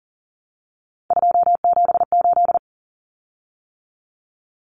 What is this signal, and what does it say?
Morse code, heard as a keyed tone, about 40 wpm, 700 Hz. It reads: 278
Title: Common Cut Numbers - 3-Digit 40wpm